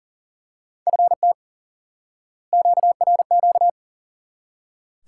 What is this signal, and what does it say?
Morse code, heard as a keyed tone, about 40 wpm, 700 Hz. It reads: FT QRQ
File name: IQE Course 3 - Lesson 89 - YAGI - Two Elements - 40wpm